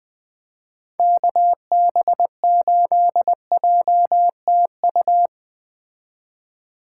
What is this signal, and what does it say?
Morse code, heard as a keyed tone, about 20 wpm, 700 Hz. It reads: KB8JTU
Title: Speed Racing - USA 2x3 Call Signs 17wpm